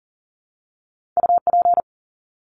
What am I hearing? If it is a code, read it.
Morse code, heard as a keyed tone, about 40 wpm, 700 Hz. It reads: UP
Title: Top 300 Words 40wpm